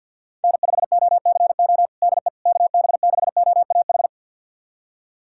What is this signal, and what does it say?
Morse code, heard as a keyed tone, about 48 wpm, 700 Hz. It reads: N5OCO DE KB6OAH
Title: USA Callsign DE Callsign 50wpm